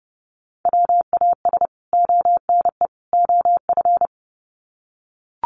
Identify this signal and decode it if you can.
Morse code, heard as a keyed tone, about 30 wpm, 700 Hz. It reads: WAS ONE OF
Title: Sets of 3 Words 30wpm